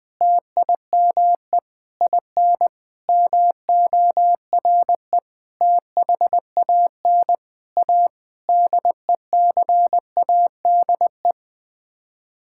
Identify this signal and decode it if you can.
Morse code, heard as a keyed tone, about 20 wpm, 700 Hz. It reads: TIME IN MORE THAN A DECADE
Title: Sets of 9 Words - 20wpm